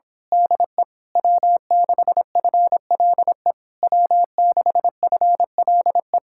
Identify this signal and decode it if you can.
Morse code, heard as a keyed tone, about 26 wpm, 700 Hz. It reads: DE W6FLE W6FLE
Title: Speed-Racing Common QSO Phrases - 22wpm